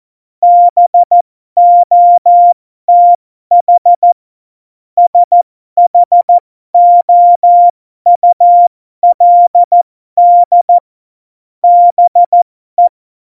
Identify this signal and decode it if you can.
Morse code, heard as a keyed tone, about 14 wpm, spaced as for 12 wpm, 700 Hz. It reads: BOTH SHOULD BE
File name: Morsekurs / Witze